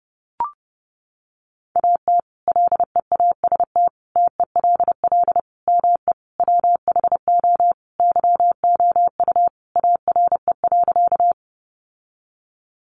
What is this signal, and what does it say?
Morse code, heard as a keyed tone, about 30 wpm, 700 Hz. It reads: AT LEAST TELL ME WHO YOU ARE.
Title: Sentences from Top 700 Words 30wpm